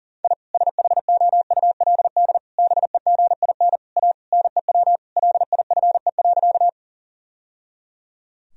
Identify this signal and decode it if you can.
Morse code, heard as a keyed tone, about 40 wpm, 700 Hz. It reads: I SHOULD BEGIN A NEW LIFE.
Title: Sentences from Top 1000 Words - Review Part 1 - 40wpm